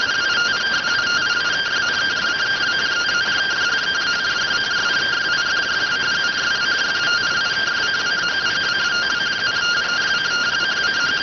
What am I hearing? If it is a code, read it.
Two-tone teleprinter signal, about 170 Hz shift.